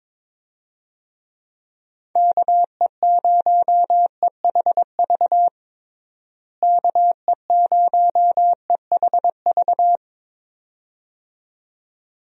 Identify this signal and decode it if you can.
Morse code, heard as a keyed tone, about 22 wpm, 700 Hz. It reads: KE0EHV KE0EHV
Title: USA Call Signs - 6 Characters - Repeated 2x - 22wpm